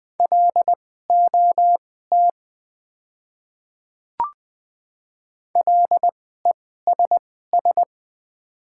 Morse code, heard as a keyed tone, about 20 wpm, spaced as for 15 wpm, 700 Hz. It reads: LOT LESS
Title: 2x Character Spacing - Top 500 Words - 20wpm